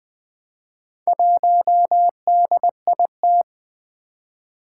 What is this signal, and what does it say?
Morse code, heard as a keyed tone, about 20 wpm, 700 Hz. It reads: 1DIT
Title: Four Challenging Letter-Number Combinations 20wpm